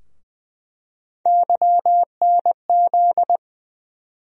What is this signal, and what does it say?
Morse code, heard as a keyed tone, about 20 wpm, 700 Hz. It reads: YNZ
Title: USA Call Sign Postfixes - 3 Letters 20wpm